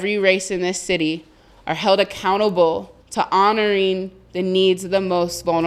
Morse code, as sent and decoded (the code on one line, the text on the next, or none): none